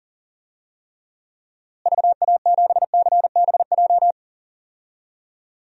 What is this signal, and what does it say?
Morse code, heard as a keyed tone, about 40 wpm, 700 Hz. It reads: VA7CBJ
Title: DXCC-100 Call Signs - 6 Characters - Repeated 3x - 40wpm